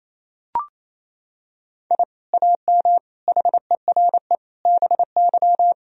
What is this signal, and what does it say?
Morse code, heard as a keyed tone, about 28 wpm, 700 Hz. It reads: I AM HERE BY
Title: Sentences from Top 1000 Words - Review Part 3 - 28wpm